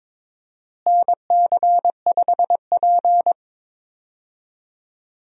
Morse code, heard as a keyed tone, about 21 wpm, 700 Hz. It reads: NC5P
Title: USA Call Signs - 4 Characters - Repeated 2x - 22wpm